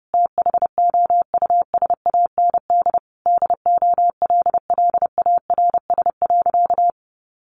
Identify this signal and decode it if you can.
Morse code, heard as a keyed tone, about 30 wpm, 700 Hz. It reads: THOUSAND DOLLARS.
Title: A Sound Of Thunder 30wpm